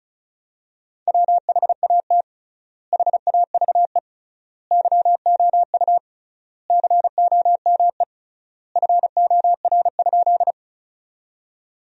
Morse code, heard as a keyed tone, about 36 wpm, 700 Hz. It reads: WHAT HAVE YOU COME FOR?
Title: 3x Word-Spacing - Sentences from Top 100 Words - 35wpm